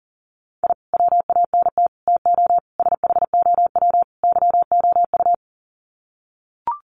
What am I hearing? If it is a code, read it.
Morse code, heard as a keyed tone, about 37 wpm, 700 Hz. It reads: I WANT TO SHOW YOU
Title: Sets of 5 Words 40wpm